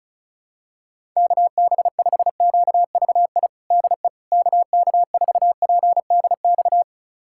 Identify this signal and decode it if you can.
Morse code, heard as a keyed tone, about 34 wpm, 700 Hz. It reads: KB5QVI DE KK4PDX
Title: USA Callsign DE Callsign - 6 Characters - 35wpm